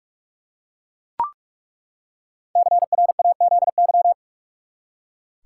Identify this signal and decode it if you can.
Morse code, heard as a keyed tone, about 42 wpm, 700 Hz. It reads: CRAZY
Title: Five Letter Words - 45wpm